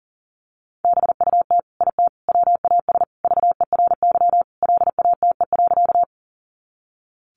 Morse code, heard as a keyed tone, about 40 wpm, 700 Hz. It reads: BUT IT WAS VERY LATE.
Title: Sentences from Top 300 Words 40wpm